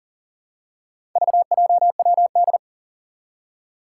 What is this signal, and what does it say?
Morse code, heard as a keyed tone, about 37 wpm, 700 Hz. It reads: VJWD
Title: Four Challenging Letter-Number Combinations 40wpm